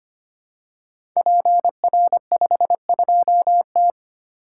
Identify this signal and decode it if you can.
Morse code, heard as a keyed tone, about 25 wpm, 700 Hz. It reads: PR52T
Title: Code Groups - 5 Letter:Numbers - 25wpm